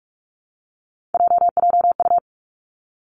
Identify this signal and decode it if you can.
Morse code, heard as a keyed tone, about 46 wpm, 700 Hz. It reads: JJU
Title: Three Challenging Letter-Number Combinations 45wpm